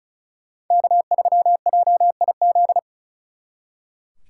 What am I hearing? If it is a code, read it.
Morse code, heard as a keyed tone, about 45 wpm, 700 Hz. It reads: K3JIZ